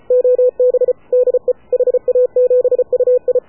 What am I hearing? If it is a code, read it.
Morse code, heard as a keyed tone, about 33 wpm, 500 Hz. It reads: OB DE HA7UI
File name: AUDIO2019 / HA7UI CW